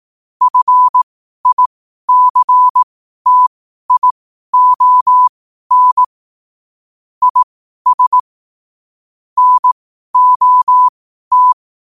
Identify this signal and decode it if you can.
Morse code, heard as a keyed tone, about 18 wpm, spaced as for 12 wpm, 1000 Hz. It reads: FICTION IS NOT
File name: qotdfiles / 12wpm